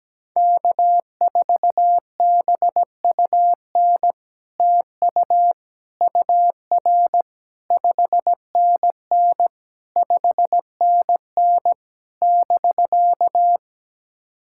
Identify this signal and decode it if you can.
Morse code, heard as a keyed tone, about 17 wpm, 700 Hz. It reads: K4BUN TU UR 5NN 5NN <BK>
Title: SOTA QSO Practice 17wpm